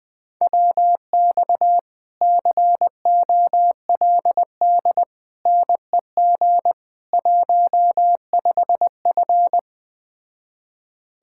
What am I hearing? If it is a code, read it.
Morse code, heard as a keyed tone, about 20 wpm, 700 Hz. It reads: WX COLD NEG 15F